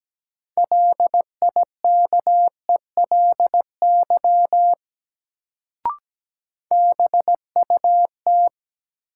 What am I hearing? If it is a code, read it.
Morse code, heard as a keyed tone, about 17 wpm, 700 Hz. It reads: LIKELY BUT
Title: Sets of 2 Words - New Word Drill - 17wpm